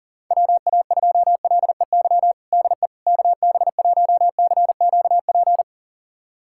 Morse code, heard as a keyed tone, about 40 wpm, 700 Hz. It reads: WA2LEY DE KB1CQP
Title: USA Callsign DE Callsign 40wpm